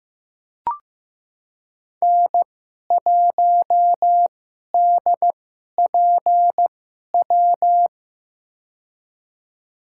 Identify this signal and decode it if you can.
Morse code, heard as a keyed tone, about 15 wpm, spaced as for 11 wpm, 700 Hz. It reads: N1DPW
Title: USA Call Signs - 5 Characters - 2x Character Spacing - 15wpm